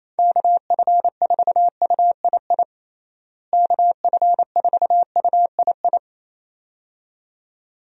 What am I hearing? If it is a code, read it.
Morse code, heard as a keyed tone, about 28 wpm, 700 Hz. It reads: KF4UII KF4UII
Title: USA Call Signs - 6 Characters - Repeated 2x - 28wpm